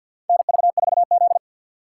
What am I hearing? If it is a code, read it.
Morse code, heard as a keyed tone, about 48 wpm, 700 Hz. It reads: NV4Z
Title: USA Call Signs - 6x Character Spacing - 50wpm